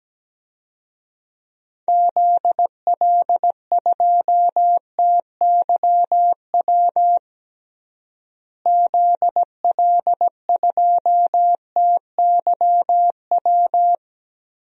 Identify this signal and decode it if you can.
Morse code, heard as a keyed tone, about 17 wpm, 700 Hz. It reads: ZL2TYW ZL2TYW
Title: DXCC-100 Call Signs - 6 Characters - Repeated 2x - 17wpm